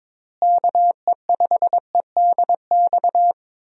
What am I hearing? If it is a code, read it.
Morse code, heard as a keyed tone, about 22 wpm, 700 Hz. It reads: KE5EDX
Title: USA Call Signs - 6 Characters - Repeated 2x - 22wpm